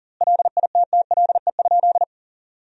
Morse code, heard as a keyed tone, about 40 wpm, 700 Hz. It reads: LITTLE?
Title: 2x Word-Spacing - Sentences from Top 500 Words - 40wpm